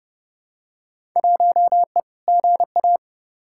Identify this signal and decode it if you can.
Morse code, heard as a keyed tone, about 32 wpm, 700 Hz. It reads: 1E GA
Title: Field Day Exchanges - 30wpm